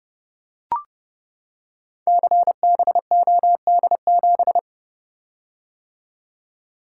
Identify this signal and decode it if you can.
Morse code, heard as a keyed tone, about 28 wpm, 700 Hz. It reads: CBOD7
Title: Code Groups - 5 Letter:Numbers - 30wpm